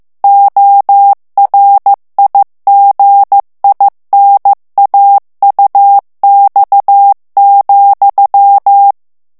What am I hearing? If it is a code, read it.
Morse code, heard as a keyed tone, about 15 wpm, 800 Hz. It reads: ORIGINAUX,
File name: FRANCAIS TEXTE CLAIR 2 VITESSE 14 MOTS MINUTES